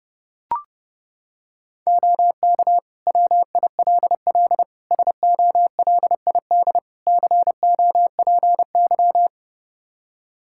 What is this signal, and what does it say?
Morse code, heard as a keyed tone, about 32 wpm, 700 Hz. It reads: OK WILL SOLID COPY
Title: Practice QSO - Exchange 3 - More Info - 30wpm